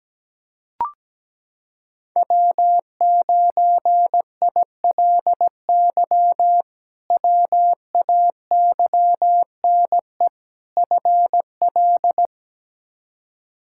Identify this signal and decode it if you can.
Morse code, heard as a keyed tone, about 17 wpm, 700 Hz. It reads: W9ILY WAYNE FL
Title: K1USN SST QSO Practice - 17wpm